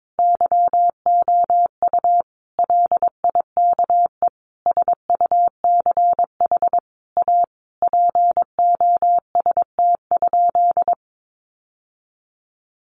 Morse code, heard as a keyed tone, about 22 wpm, 700 Hz. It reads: YOU LIKE SUCH A POST?
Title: Sentences from Top 1000 Words 22wpm